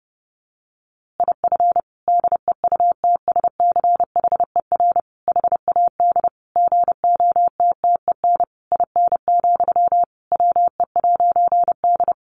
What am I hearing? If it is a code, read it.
Morse code, heard as a keyed tone, about 30 wpm, 700 Hz. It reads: IF DEUTSCHER HAD GOTTEN IN, WE'D
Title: A Sound Of Thunder 30wpm